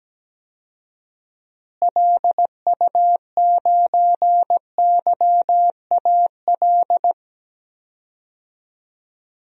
Morse code, heard as a keyed tone, about 17 wpm, 700 Hz. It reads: LU9YAL
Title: DXCC-100 Call Signs - 6 Characters - Repeated 3x - 17wpm